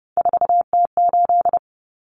Morse code, heard as a keyed tone, about 30 wpm, 700 Hz. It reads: 4T8
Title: Common Cut Numbers - 3-Digit 30wpm